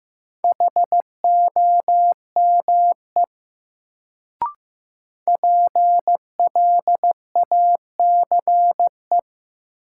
Morse code, heard as a keyed tone, about 15 wpm, 700 Hz. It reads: HOME PLACE
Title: Sentences from Top 200 Words - New Word Drill - 15wpm